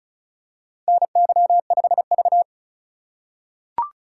Morse code, heard as a keyed tone, about 34 wpm, 700 Hz. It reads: NY5V
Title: USA Call Signs - 4 Characters - 3x Character Spacing - 35wpm